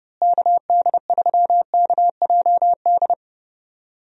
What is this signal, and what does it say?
Morse code, heard as a keyed tone, about 29 wpm, 700 Hz. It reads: KD3KJD
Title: USA Call Signs - 6 Characters - Repeated 2x - 30wpm